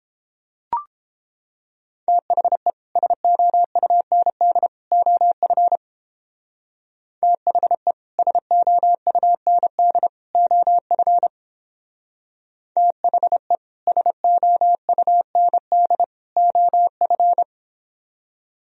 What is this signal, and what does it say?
Morse code, heard as a keyed tone, about 29 wpm, 700 Hz. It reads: THE SOUND OF THE SOUND OF THE SOUND OF